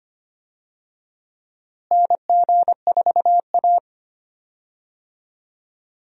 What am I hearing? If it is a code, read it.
Morse code, heard as a keyed tone, about 25 wpm, 700 Hz. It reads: NG4A
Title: USA Call Signs - 4 Characters - Repeated 3x - 25wpm